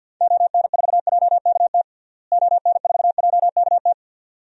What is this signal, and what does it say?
Morse code, heard as a keyed tone, about 53 wpm, 700 Hz. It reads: ON4JKT ON4JKT
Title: DXCC-100 Call Signs - 6 Characters - Repeated 2x - 50wpm